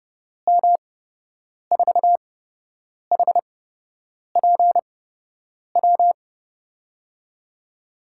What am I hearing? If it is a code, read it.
Morse code, heard as a keyed tone, about 30 wpm, spaced as for 8 wpm, 700 Hz. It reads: M4HPW
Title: USA Call Signs - 6 Characters - 8x Character Spacing - 30wpm